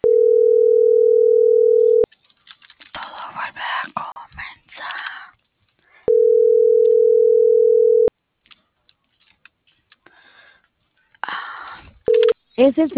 Ringback tone (North America), 440 + 480 Hz.